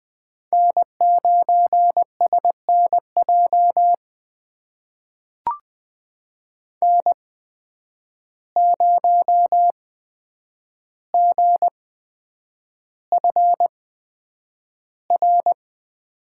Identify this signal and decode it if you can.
Morse code, heard as a keyed tone, about 20 wpm, spaced as for 9 wpm, 700 Hz. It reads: N9SNJ N0GFR
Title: USA Call Signs - 5 Characters - 8x Character Spacing - 20wpm